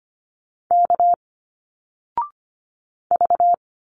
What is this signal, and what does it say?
Morse code, heard as a keyed tone, about 25 wpm, 700 Hz. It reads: K V
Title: Single Letters 25wpm